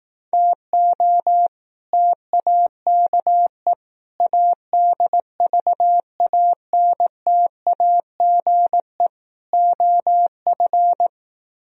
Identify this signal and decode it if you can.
Morse code, heard as a keyed tone, about 18 wpm, 700 Hz. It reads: TO TAKE ADVANTAGE OF
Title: Speed-Racing - Sets of 4 Words 15wpm